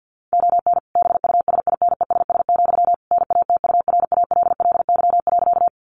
Morse code, heard as a keyed tone, about 50 wpm, 700 Hz. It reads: ON BUSINESS, NATURALLY.
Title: Sentences from Top 1300 Words 50wpm